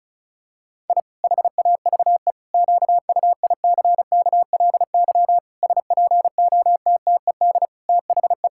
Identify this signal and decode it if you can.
Morse code, heard as a keyed tone, about 36 wpm, 700 Hz. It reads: I HAVE QUICKLY SPOTTED THE
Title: Pangram Challenge v2 - 35wpm